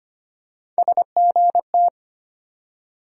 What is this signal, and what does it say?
Morse code, heard as a keyed tone, about 25 wpm, 700 Hz. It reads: SGT
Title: USA Call Sign Postfixes - 3 Letters 25wpm